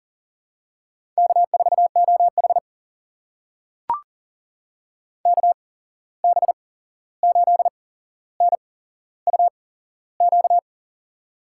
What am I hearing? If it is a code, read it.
Morse code, heard as a keyed tone, about 40 wpm, spaced as for 15 wpm, 700 Hz. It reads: K4OH KB8NUQ
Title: USA Call Signs - 8x Character Spacing - 40wpm